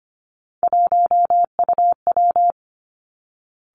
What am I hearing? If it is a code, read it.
Morse code, heard as a keyed tone, about 25 wpm, 700 Hz. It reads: 1UW
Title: Three Challenging Letter-Number Combinations 25wpm